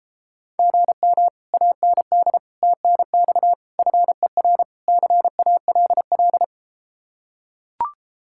Morse code, heard as a keyed tone, about 33 wpm, 700 Hz. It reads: GM AND TNX FER CALL